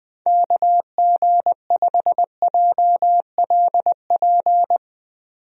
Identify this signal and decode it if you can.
Morse code, heard as a keyed tone, about 20 wpm, 700 Hz. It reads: KG5JLP
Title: USA Call Signs - 6 Characters - 20wpm